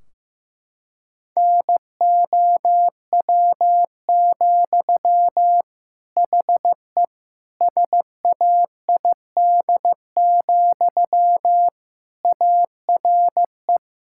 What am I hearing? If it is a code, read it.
Morse code, heard as a keyed tone, about 15 wpm, 700 Hz. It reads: NOW, HE SAID, ARE
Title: Sentences from Top 1600 Words 15wpm